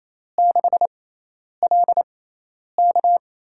Morse code, heard as a keyed tone, about 28 wpm, spaced as for 10 wpm, 700 Hz. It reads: 6LK
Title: USA Call Signs - 5 Characters - 6x Character Spacing - 28wpm